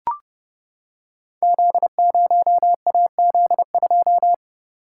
Morse code, heard as a keyed tone, about 30 wpm, 700 Hz. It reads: Z0AZ2
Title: Code Groups - 5 Letter:Numbers - 30wpm